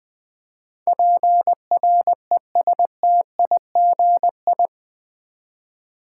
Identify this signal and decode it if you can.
Morse code, heard as a keyed tone, about 20 wpm, 700 Hz. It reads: PRESTIGI
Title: Build A Word - 4 Letters At A Time - 11-Letter Words - 20wpm